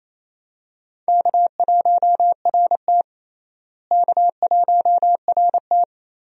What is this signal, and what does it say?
Morse code, heard as a keyed tone, about 28 wpm, 700 Hz. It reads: K1RT K1RT
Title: USA Call Signs - 4 Characters - Repeated 3x - 28wpm